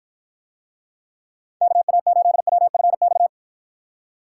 Morse code, heard as a keyed tone, about 54 wpm, 700 Hz. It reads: KA8WFX